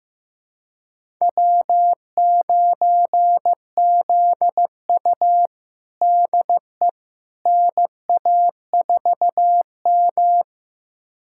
Morse code, heard as a keyed tone, about 15 wpm, 700 Hz. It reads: W9ZU DE NA4M
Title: USA Callsign DE Callsign - 4 Characters - 15wpm